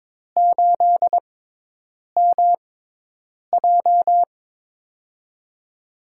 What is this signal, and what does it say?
Morse code, heard as a keyed tone, about 22 wpm, spaced as for 8 wpm, 700 Hz. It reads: 8MJ
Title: USA Call Signs - 4 Characters - 6x Character Spacing - 22wpm